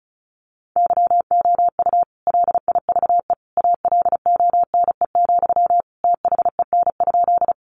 Morse code, heard as a keyed tone, about 34 wpm, 700 Hz. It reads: YOU LIVE ALONE, THEN?
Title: Sentences from Top 1000 Words 35wpm